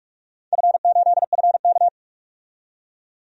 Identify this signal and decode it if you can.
Morse code, heard as a keyed tone, about 43 wpm, 700 Hz. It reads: F8FK
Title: DXCC-100 Call Signs - 4 Characters - 45wpm